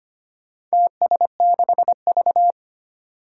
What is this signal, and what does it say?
Morse code, heard as a keyed tone, about 25 wpm, 700 Hz. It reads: TS6V
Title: Four Challenging Letter-Number Combinations 25wpm